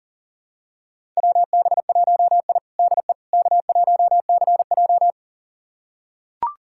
Morse code, heard as a keyed tone, about 40 wpm, 700 Hz. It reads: WB1I DE K1CJ
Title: USA Callsign DE Callsign - 4 Characters - 40wpm